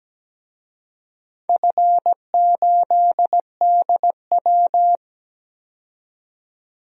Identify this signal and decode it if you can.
Morse code, heard as a keyed tone, about 17 wpm, 700 Hz. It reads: F8DW
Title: DXCC-100 Call Signs - 4 Characters - 17wpm